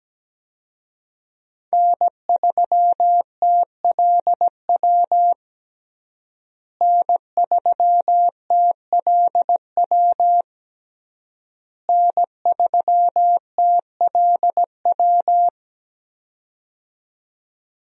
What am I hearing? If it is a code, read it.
Morse code, heard as a keyed tone, about 17 wpm, 700 Hz. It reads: N3TLW N3TLW N3TLW
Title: USA Call Signs - 5 Characters - Repeated 3x - 17wpm